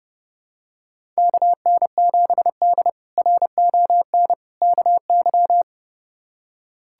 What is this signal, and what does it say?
Morse code, heard as a keyed tone, about 30 wpm, 700 Hz. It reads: KN7D RON KY
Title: K1USN SST QSO Practice - 30wpm